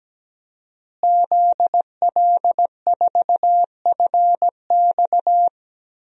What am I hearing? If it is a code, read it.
Morse code, heard as a keyed tone, about 17 wpm, 700 Hz. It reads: ZL4FX
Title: DXCC-100 Call Signs - 5 Characters - Repeated 3x - 17wpm